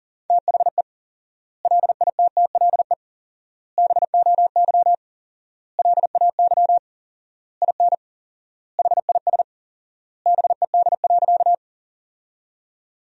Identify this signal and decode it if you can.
Morse code, heard as a keyed tone, about 37 wpm, 700 Hz. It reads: THE LITTLE BOY LAY IN HIS BED.
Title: 4x Word-Spacing - Sentences from Top 500 Words - 40wpm